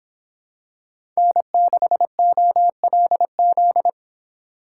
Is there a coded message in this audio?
Yes: Morse code from 1.2 to 3.9 s, heard as a keyed tone, about 26 wpm, 700 Hz. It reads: N6OLZ